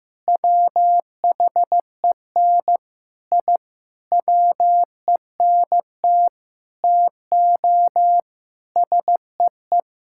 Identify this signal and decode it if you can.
Morse code, heard as a keyed tone, about 16 wpm, 700 Hz. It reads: WHEN I WENT TO SEE
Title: Sentences from Top 1000 Words - Review Part 5 - 15wpm